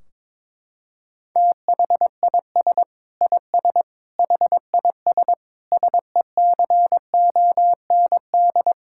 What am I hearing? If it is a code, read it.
Morse code, heard as a keyed tone, about 22 wpm, 700 Hz. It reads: THIS IS HIS SECOND
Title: Sentences from Top 1600 Words 22wpm